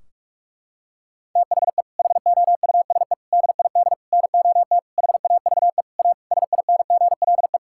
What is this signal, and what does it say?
Morse code, heard as a keyed tone, about 47 wpm, 700 Hz. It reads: THE HOUSE DID NOT HAVE A SINGLE
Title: Sentences from Top 1600 Words 45wpm